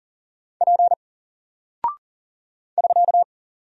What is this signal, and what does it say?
Morse code, heard as a keyed tone, about 40 wpm, 700 Hz. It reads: P <SK>
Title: Single Character (Letter:Number:Punctuation:Prosign) 40wpm